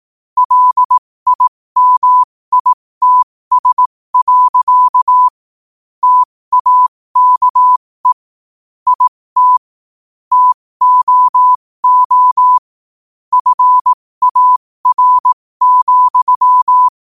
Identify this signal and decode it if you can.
Morse code, heard as a keyed tone, about 18 wpm, spaced as for 15 wpm, 1000 Hz. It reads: LIMITS. TAKE IT TOO FAR,
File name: qotdfiles / 15wpm